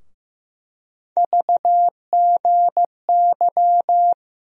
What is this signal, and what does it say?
Morse code, heard as a keyed tone, about 15 wpm, 700 Hz. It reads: VGY
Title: USA Call Sign Postfixes - 3 Letters 15wpm